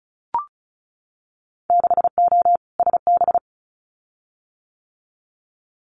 Morse code, heard as a keyed tone, about 36 wpm, 700 Hz. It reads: 6O SB
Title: Winter Field Day Exchanges Encore 35wpm